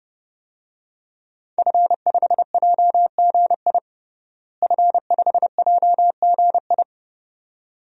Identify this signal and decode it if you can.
Morse code, heard as a keyed tone, about 31 wpm, 700 Hz. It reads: F5JGI F5JGI
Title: DXCC-100 Call Signs - 5 Characters - Repeated 2x - 30wpm